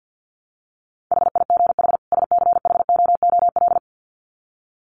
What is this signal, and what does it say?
Morse code, heard as a keyed tone, about 50 wpm, 700 Hz. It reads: HIGH SCHOOL
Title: Sets of 2 Words 50wpm